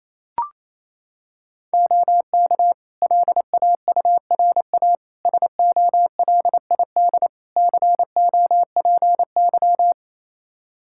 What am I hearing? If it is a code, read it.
Morse code, heard as a keyed tone, about 28 wpm, 700 Hz. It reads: OK LAURA SOLID COPY
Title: K1USN SST QSO Practice 28wpm